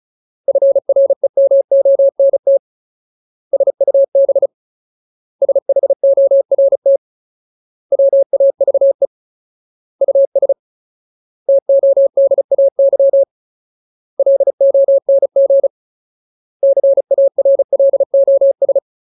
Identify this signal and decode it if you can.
Morse code, heard as a keyed tone, about 34 wpm, 550 Hz. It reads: FREMONT SUB SHORT WAVE US TODAY LONG CARLOS
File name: Ultimate Word List / 35WPM Ultimate